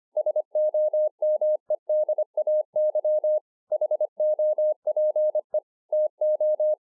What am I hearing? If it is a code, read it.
Morse code, heard as a keyed tone, about 25 wpm, 610 Hz. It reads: SOMEDAY HOPE TO